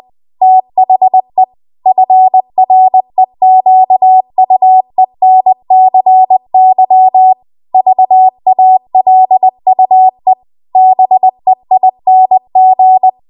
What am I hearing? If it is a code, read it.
Morse code, heard as a keyed tone, about 20 wpm, 750 Hz. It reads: THE FREQUENCY VALUE BEING
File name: Archive / 20 WPM / 20WPM